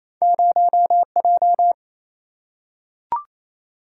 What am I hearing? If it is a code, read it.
Morse code, heard as a keyed tone, about 28 wpm, 700 Hz. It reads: 0J
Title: Single-Letter Single-Number 28wpm